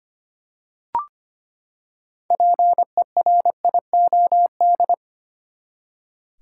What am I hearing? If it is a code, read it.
Morse code, heard as a keyed tone, about 25 wpm, 700 Hz. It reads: PERIOD
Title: Six Letter Words - 25wpm